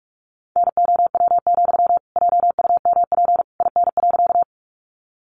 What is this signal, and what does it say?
Morse code, heard as a keyed tone, about 44 wpm, 700 Hz. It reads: NOW, JUMP IN.
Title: Sentences from Top 1300 Words 45wpm